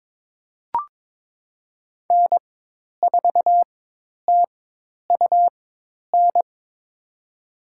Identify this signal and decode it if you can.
Morse code, heard as a keyed tone, about 22 wpm, spaced as for 10 wpm, 700 Hz. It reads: N4TUN
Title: USA Call Signs - 5 Characters - 4x Character Spacing - 22wpm